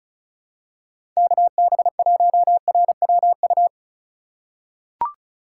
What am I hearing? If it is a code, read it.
Morse code, heard as a keyed tone, about 36 wpm, 700 Hz. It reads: KB1RWU
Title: USA Call Signs - 4x Character Spacing - 35wpm